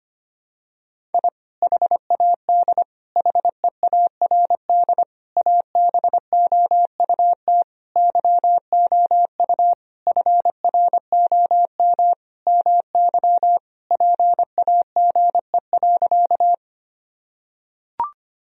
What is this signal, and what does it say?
Morse code, heard as a keyed tone, about 25 wpm, 700 Hz. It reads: I HAD HEARD ABOUT YOU FROM MY PAGE.
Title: Sentences from Top 1000 Words - Review Part 1 - 25wpm